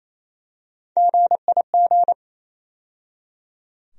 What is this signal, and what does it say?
Morse code, heard as a keyed tone, about 27 wpm, 700 Hz. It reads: GIG
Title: USA Call Sign Postfixes - 3 Letters 28wpm